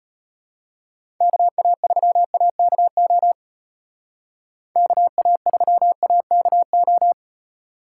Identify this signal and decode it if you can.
Morse code, heard as a keyed tone, about 32 wpm, 700 Hz. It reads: KA3AKO KA3AKO